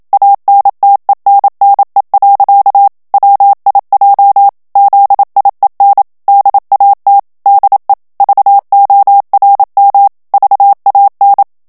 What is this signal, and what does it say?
Morse code, heard as a keyed tone, about 27 wpm, 800 Hz. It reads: ANTENNE. WIJ ZIEN DAT DE VORM VAN
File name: HOLLANDAIS texte clair 1 vitesse 26 mots minute